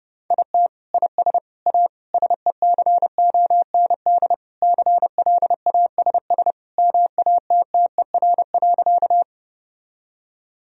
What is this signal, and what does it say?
Morse code, heard as a keyed tone, about 30 wpm, 700 Hz. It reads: IT IS A SECOND CLASS MATTER.
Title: Sentences from Top 1000 Words - Review Part 2 - 30wpm